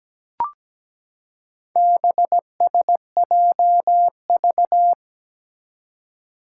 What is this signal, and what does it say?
Morse code, heard as a keyed tone, about 17 wpm, 700 Hz. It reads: BSJV
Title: Four Challenging Letter-Number Combinations 17wpm